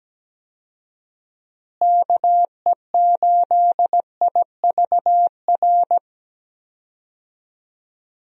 Morse code, heard as a keyed tone, about 17 wpm, 700 Hz. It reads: KE8IVR
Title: USA Call Signs - 6 Characters - Repeated 2x - 17wpm